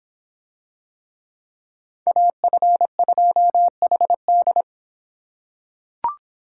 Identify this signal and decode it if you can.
Morse code, heard as a keyed tone, about 24 wpm, 700 Hz. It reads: AF2HD